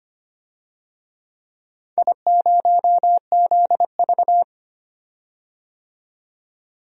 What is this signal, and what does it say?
Morse code, heard as a keyed tone, about 25 wpm, 700 Hz. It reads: I0ZV
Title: DXCC-100 Call Signs - 4 Characters - 25wpm